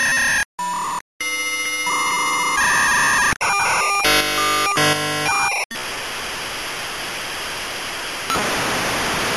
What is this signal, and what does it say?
Dial-up modem handshake.
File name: Modem Sounds / sm56 tac v90